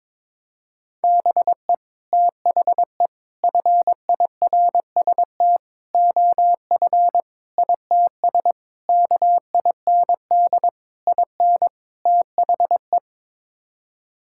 Morse code, heard as a keyed tone, about 21 wpm, 700 Hz. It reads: BE THE FIRST OF ITS KIND IN THE
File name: practice / 22wpm / Sets of 8 Words - 22wpm